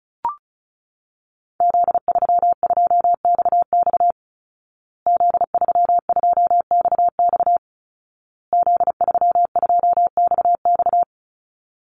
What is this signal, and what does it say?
Morse code, heard as a keyed tone, about 34 wpm, 700 Hz. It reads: Z32XX Z32XX Z32XX
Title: Call Signs Repeated 2x - 35wpm